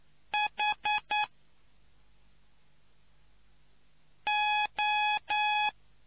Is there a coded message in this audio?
Yes: Morse code from 0.3 to 5.7 s, heard as a keyed tone, about 9 wpm, 840 Hz. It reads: H O